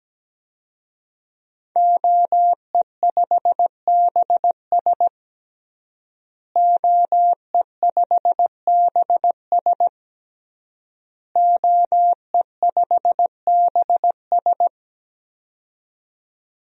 Morse code, heard as a keyed tone, about 17 wpm, 700 Hz. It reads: OE5BS OE5BS OE5BS
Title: DXCC-100 Call Signs - 5 Characters - Repeated 3x - 17wpm